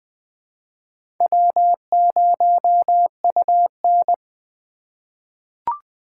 Morse code, heard as a keyed tone, about 20 wpm, 700 Hz. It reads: W0UN
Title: USA Call Signs - 4 Characters - 4x Character Spacing - 20wpm